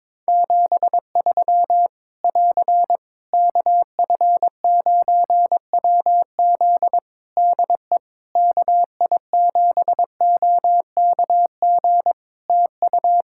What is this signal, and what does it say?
Morse code, heard as a keyed tone, about 22 wpm, 700 Hz. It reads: 73 <AR> KF9WZ DE KI7OKG TU
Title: K1USN SST QSO Practice 22wpm